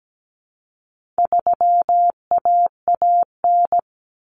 Morse code, heard as a keyed tone, about 17 wpm, 700 Hz. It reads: 3AAN
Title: Common Cut Numbers - 4-Digit 17wpm